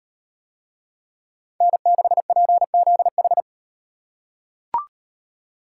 Morse code, heard as a keyed tone, about 38 wpm, 700 Hz. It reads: N6PZH